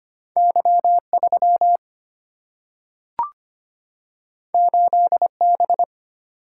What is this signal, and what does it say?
Morse code, heard as a keyed tone, about 25 wpm, 700 Hz. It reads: Y 3 8 B
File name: Two Letter-Number Combinations - 25wpm